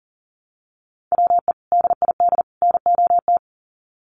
Morse code, heard as a keyed tone, about 40 wpm, 700 Hz. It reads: WE DID NOT
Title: Sets of 3 Words 40wpm